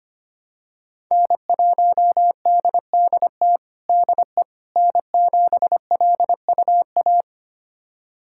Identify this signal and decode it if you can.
Morse code, heard as a keyed tone, about 25 wpm, 700 Hz. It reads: N1DDT DE N7LUA
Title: USA Callsign DE Callsign - 5 Characters - 25wpm